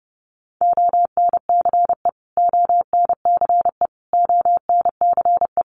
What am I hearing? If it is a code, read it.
Morse code, heard as a keyed tone, about 30 wpm, 700 Hz. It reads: ONCE ONCE ONCE
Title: Top 500 Words Repeated 30wpm